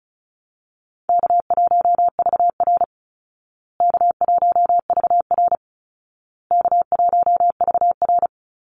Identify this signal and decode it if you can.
Morse code, heard as a keyed tone, about 34 wpm, 700 Hz. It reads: K1VR K1VR K1VR
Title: Call Signs Repeated 2x - 35wpm